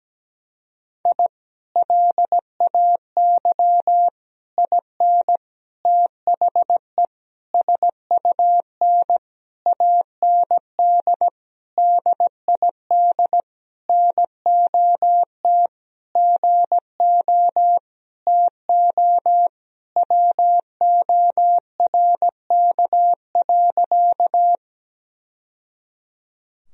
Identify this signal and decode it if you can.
Morse code, heard as a keyed tone, about 17 wpm, 700 Hz. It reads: I LAY IN THE SUN AND DID NOT GO TO WORK.
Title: Sentences from Top 1000 Words - Review Part 2 - 17wpm